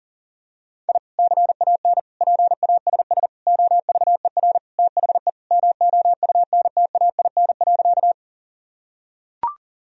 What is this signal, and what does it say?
Morse code, heard as a keyed tone, about 37 wpm, 700 Hz. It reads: I CAN PASS OVER THE MOUNTAIN.
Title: Sentences from Top 1000 Words - Review Part 1 - 40wpm